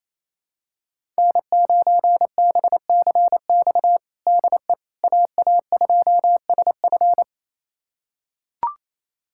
Morse code, heard as a keyed tone, about 28 wpm, 700 Hz. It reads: N9BCX DE AA2SF
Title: USA Callsign DE Callsign - 5 Characters - 28wpm